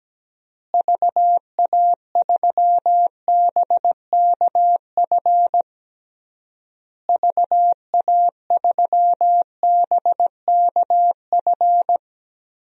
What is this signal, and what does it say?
Morse code, heard as a keyed tone, about 17 wpm, 700 Hz. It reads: VA3BKF VA3BKF
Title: DXCC-100 Call Signs - 6 Characters - Repeated 3x - 17wpm